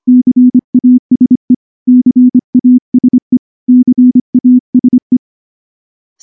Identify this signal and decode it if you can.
Morse code, heard as a keyed tone, about 24 wpm, 250 Hz. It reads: CASE CASE CASE
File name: morse / morse-training-25WPM